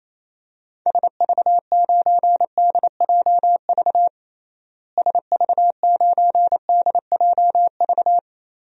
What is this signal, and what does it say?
Morse code, heard as a keyed tone, about 28 wpm, 700 Hz. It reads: SV9DJV SV9DJV
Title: DXCC-100 Call Signs - 6 Characters - Repeated 2x - 28wpm